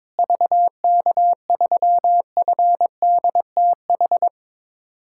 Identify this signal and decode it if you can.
Morse code, heard as a keyed tone, about 22 wpm, 700 Hz. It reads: VK3FDTH
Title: USA Callsign DE Callsign 22wpm